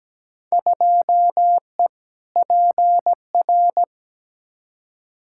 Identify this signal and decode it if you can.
Morse code, heard as a keyed tone, about 17 wpm, 700 Hz. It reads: 2E PR
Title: Field Day Exchanges Encore - 17wpm